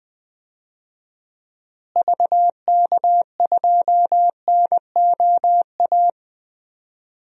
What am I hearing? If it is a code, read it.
Morse code, heard as a keyed tone, about 21 wpm, 700 Hz. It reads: VK2NOA
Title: DXCC-100 Call Signs - 6 Characters - Repeated 2x - 20wpm